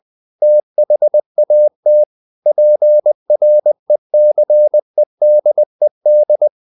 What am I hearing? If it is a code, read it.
Morse code, heard as a keyed tone, about 20 wpm, 600 Hz. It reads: THAT PRECEDED